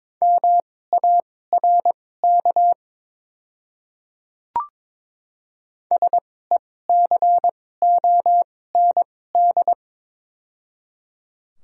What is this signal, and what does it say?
Morse code, heard as a keyed tone, about 23 wpm, spaced as for 16 wpm, 700 Hz. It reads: MARK SECOND
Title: 2x Character Spacing - Top 500 Words - 22wpm